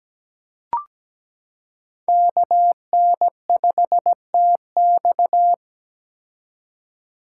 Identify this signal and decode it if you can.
Morse code, heard as a keyed tone, about 17 wpm, 700 Hz. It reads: KN5TX
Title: K1USN SST QSO Practice - 17wpm